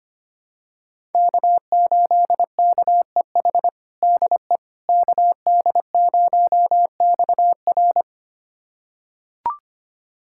Morse code, heard as a keyed tone, about 25 wpm, 700 Hz. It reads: K8KEH DE KD0XR
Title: USA Callsign DE Callsign - 5 Characters - 25wpm